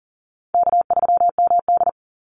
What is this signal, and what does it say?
Morse code, heard as a keyed tone, about 40 wpm, 700 Hz. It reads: K3MD
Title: Call Signs Repeated 3x - 40wpm